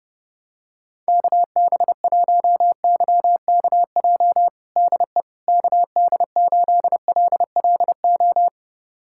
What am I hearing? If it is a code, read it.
Morse code, heard as a keyed tone, about 29 wpm, 700 Hz. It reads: KB1YKJ DE KD8LLO
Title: USA Callsign DE Callsign - 6 Characters - 30wpm